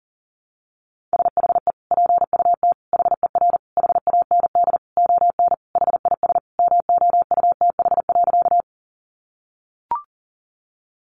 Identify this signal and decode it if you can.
Morse code, heard as a keyed tone, about 40 wpm, 700 Hz. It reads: SHE PUT HER HAND ON HIS MOUTH.
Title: Sentences from Top 1500 Words 40wpm